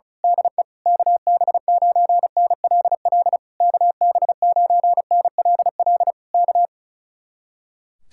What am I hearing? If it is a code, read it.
Morse code, heard as a keyed tone, about 32 wpm, 700 Hz. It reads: DE KB9NLL KB9NLL K
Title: Speed-Racing Common QSO Phrases Encore - 35wpm